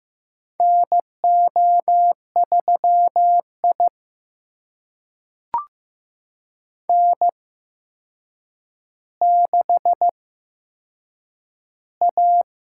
Morse code, heard as a keyed tone, about 15 wpm, spaced as for 7 wpm, 700 Hz. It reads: NO3I N6A
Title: USA Call Signs - 4 Characters - 8x Character Spacing - 15wpm